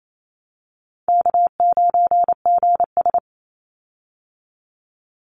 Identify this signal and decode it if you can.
Morse code, heard as a keyed tone, about 28 wpm, 700 Hz. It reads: K9GS
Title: Call Signs Repeated 2x - 28wpm